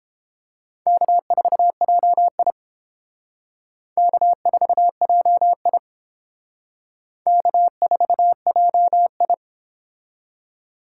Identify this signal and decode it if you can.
Morse code, heard as a keyed tone, about 29 wpm, 700 Hz. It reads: K4JI K4JI K4JI